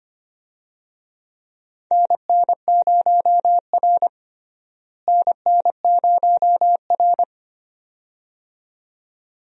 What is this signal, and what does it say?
Morse code, heard as a keyed tone, about 25 wpm, 700 Hz. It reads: NN0R NN0R
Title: USA Call Signs - 4 Characters - Repeated 2x - 25wpm